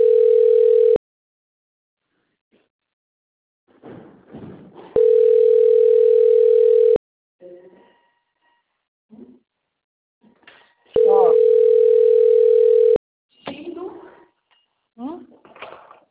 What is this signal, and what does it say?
Ringback tone (North America), 440 + 480 Hz, with speech.
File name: rg-1600-unknown- - - .49514